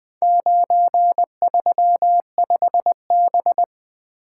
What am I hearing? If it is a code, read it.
Morse code, heard as a keyed tone, about 20 wpm, 700 Hz. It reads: 935B
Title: Four Letter-Number Combinations - 20wpm